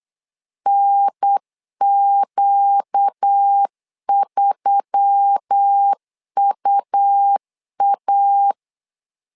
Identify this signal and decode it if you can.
Morse code, heard as a keyed tone, about 8 wpm, 800 Hz. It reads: NQ3UA
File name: LEZIONI / 4 Q1S3 8wpm 3037